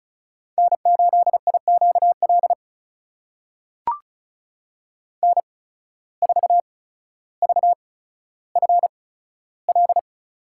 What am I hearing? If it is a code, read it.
Morse code, heard as a keyed tone, about 36 wpm, spaced as for 15 wpm, 700 Hz. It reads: N8IQL N4VFL
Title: USA Call Signs - 5 Characters - 8x Character Spacing - 35wpm